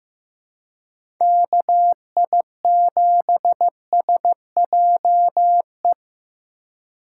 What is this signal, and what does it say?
Morse code, heard as a keyed tone, about 15 wpm, 700 Hz. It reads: KI7SJE
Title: USA Call Signs - 6 Characters - 2x Character Spacing - 15wpm